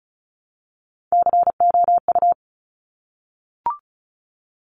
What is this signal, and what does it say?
Morse code, heard as a keyed tone, about 36 wpm, 700 Hz. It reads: COU
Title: Top Three Letter Combinations in Words 35wpm